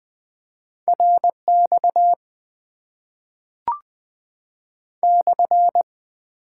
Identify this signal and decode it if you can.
Morse code, heard as a keyed tone, about 20 wpm, 700 Hz. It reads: RX /
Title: Top 90 QSO Elements - 8x Character Spacing - 20wpm